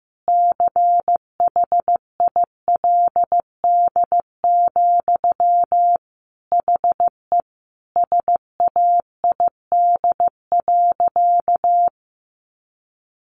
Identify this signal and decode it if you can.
Morse code, heard as a keyed tone, about 15 wpm, 700 Hz. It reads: CHILD, HE SAID.
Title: Sentences from Top 800 Words 15wpm